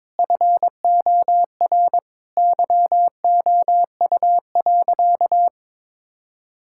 Morse code, heard as a keyed tone, about 21 wpm, 700 Hz. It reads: FOR YOU.
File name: practice / 22wpm / Sentences with 8-Word Phrases - 22wpm